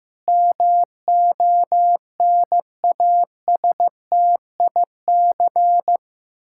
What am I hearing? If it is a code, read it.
Morse code, heard as a keyed tone, about 14 wpm, 700 Hz. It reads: MONASTIC
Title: Pangram Challenge v2 - 15wpm